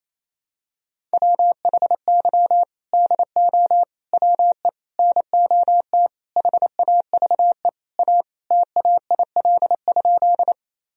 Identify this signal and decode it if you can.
Morse code, heard as a keyed tone, about 28 wpm, 700 Hz. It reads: WHY DO WE NOT HAVE A TAIL?
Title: Sentences from Top 1000 Words - Review Part 3 - 28wpm